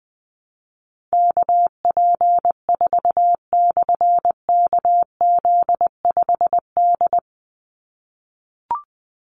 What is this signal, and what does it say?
Morse code, heard as a keyed tone, about 20 wpm, 700 Hz. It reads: KP4/KZ5D
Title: Call Signs - 20wpm